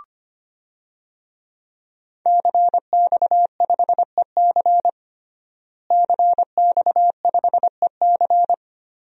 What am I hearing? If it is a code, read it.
Morse code, heard as a keyed tone, about 25 wpm, 700 Hz. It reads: CX5EC CX5EC
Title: DXCC-100 Call Signs - 5 Characters - Repeated 3x - 25wpm